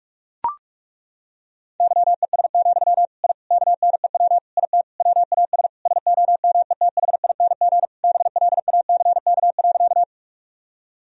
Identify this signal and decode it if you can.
Morse code, heard as a keyed tone, about 41 wpm, 700 Hz. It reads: YES, I KNEW IT WAS SOMETHING BLACK.
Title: Sentences from Top 1000 Words - Review Part 2 - 45wpm